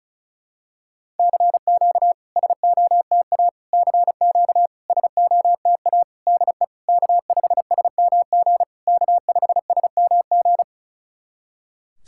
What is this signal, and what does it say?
Morse code, heard as a keyed tone, about 36 wpm, 660 Hz. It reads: CQ SOTA CQ SOTA DE K5SMG K5SMG
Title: SOTA QSO Practice 35wpm